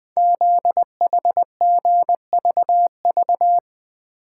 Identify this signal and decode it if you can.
Morse code, heard as a keyed tone, about 20 wpm, 700 Hz. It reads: ZHGVV
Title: Code Groups - 5 Letters - 20wpm